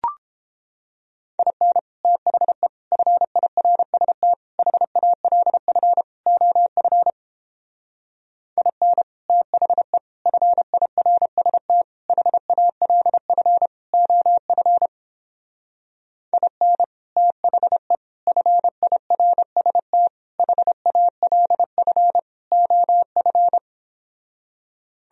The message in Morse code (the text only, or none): IN THE FIRST HALF OF IN THE FIRST HALF OF IN THE FIRST HALF OF